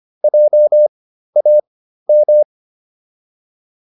Morse code, heard as a keyed tone, about 25 wpm, spaced as for 13 wpm, 600 Hz. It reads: JAM